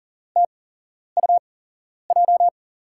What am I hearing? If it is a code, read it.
Morse code, heard as a keyed tone, about 40 wpm, spaced as for 11 wpm, 700 Hz. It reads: TUJ
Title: USA Call Signs - 8x Character Spacing - 40wpm